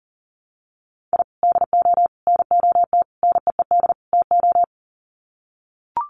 Morse code, heard as a keyed tone, about 40 wpm, 700 Hz. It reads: I DO NOT NEED TO
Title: Sets of 5 Words 40wpm